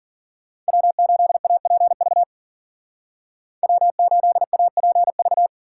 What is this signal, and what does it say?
Morse code, heard as a keyed tone, about 49 wpm, 700 Hz. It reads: W8APV W8APV
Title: Speed Racing - USA 1x3 Call Signs 40wpm